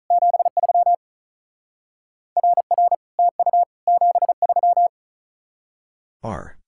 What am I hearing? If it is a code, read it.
Morse code, heard as a keyed tone, about 44 wpm, 700 Hz. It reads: 73 RR TU 73
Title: Speed-Racing Common QSO Phrases Encore - 35wpm